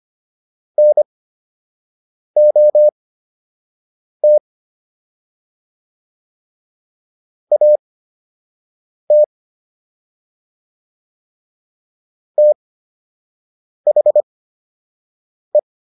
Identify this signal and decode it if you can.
Morse code, heard as a keyed tone, about 25 wpm, spaced as for 6 wpm, 600 Hz. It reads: NOT AT THE